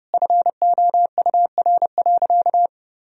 Morse code, heard as a keyed tone, about 32 wpm, 700 Hz. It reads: FOUR.
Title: Sentences from Top 1000 Words - Review Part 1 - 30wpm